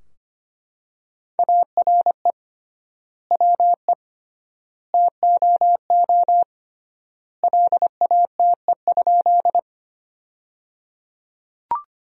Morse code, heard as a keyed tone, about 25 wpm, 700 Hz. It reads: ARE WE TOO LATE?
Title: 3x Word-Spacing - Sentences from Top 300 Words - 25wpm